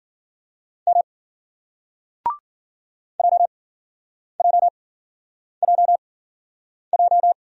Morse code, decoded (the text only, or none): M JJJJ